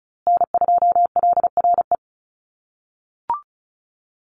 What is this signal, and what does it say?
Morse code, heard as a keyed tone, about 34 wpm, 700 Hz. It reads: N2LRE
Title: USA Call Signs Repeated 3x 35wpm